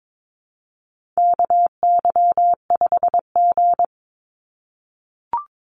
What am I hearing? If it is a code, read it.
Morse code, heard as a keyed tone, about 22 wpm, 700 Hz. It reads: KY5G
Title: USA Call Signs 22wpm